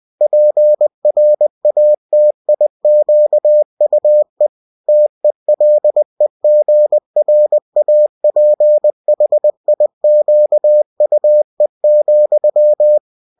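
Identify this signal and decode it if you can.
Morse code, heard as a keyed tone, about 20 wpm, 600 Hz. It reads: PRATIQUE TELEGRAPHIQUE,
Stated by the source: UFT bulletin sent in CW.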